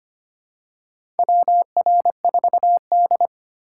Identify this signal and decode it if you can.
Morse code, heard as a keyed tone, about 25 wpm, 700 Hz. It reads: WR4D
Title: USA Call Signs - 4 Characters - 3x Character Spacing - 25wpm